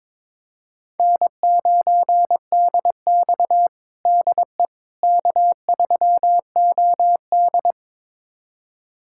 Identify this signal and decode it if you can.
Morse code, heard as a keyed tone, about 22 wpm, 700 Hz. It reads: N9DX DE K3OD
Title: USA Callsign DE Callsign - 4 Characters - 22wpm